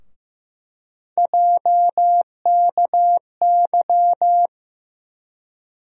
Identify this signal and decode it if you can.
Morse code, heard as a keyed tone, about 15 wpm, 700 Hz. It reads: JKY
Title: USA Call Sign Postfixes - 3 Letters 15wpm